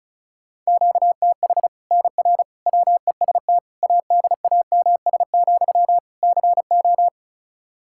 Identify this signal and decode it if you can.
Morse code, heard as a keyed tone, about 36 wpm, 700 Hz. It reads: QTH NR WEST ADAMS, CO
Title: Practice QSO - Exchange 1 - Beginning - 35wpm